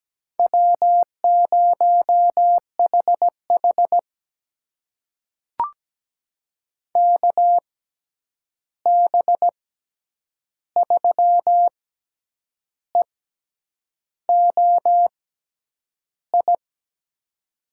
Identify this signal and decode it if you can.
Morse code, heard as a keyed tone, about 17 wpm, spaced as for 8 wpm, 700 Hz. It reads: W0HH KB3EOI
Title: USA Call Signs - 6x Character Spacing - 17wpm